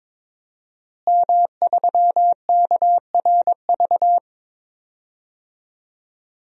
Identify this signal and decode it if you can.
Morse code, heard as a keyed tone, about 22 wpm, 700 Hz. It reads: M3KRV
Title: DXCC-100 Call Signs - 5 Characters - Repeated 2x - 22wpm